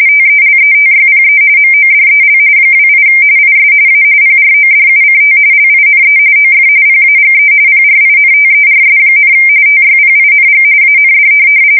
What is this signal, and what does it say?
RTTY, about 45.45 baud. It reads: TELETYPE MODEL 28 IS A PRODUCT LINE OF PAGE PRINTERS, TYPING AND
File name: fsk170 / teletype-corp